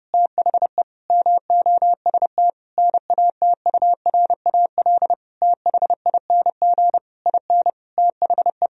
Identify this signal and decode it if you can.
Morse code, heard as a keyed tone, about 28 wpm, 700 Hz. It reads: THE MOST NATURAL THING IN THE
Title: Sentences from Top 1000 Words - Review Part 4 - 30wpm